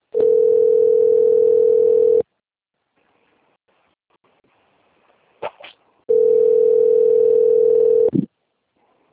Ringback tone (North America), 440 + 480 Hz.